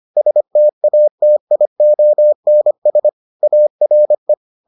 Morse code, heard as a keyed tone, about 25 wpm, 600 Hz. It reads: STATIONS ARE